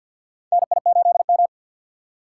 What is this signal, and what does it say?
Morse code, heard as a keyed tone, about 47 wpm, 700 Hz. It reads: NI8M
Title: USA Call Signs - 4 Characters - 6x Character Spacing - 50wpm